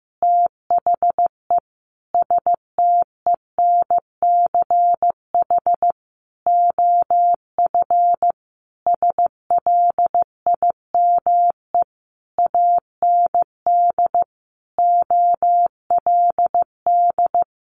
Morse code, heard as a keyed tone, about 15 wpm, 700 Hz. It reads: THE STENCH OF SLIME AND OLD
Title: A Sound Of Thunder 15wpm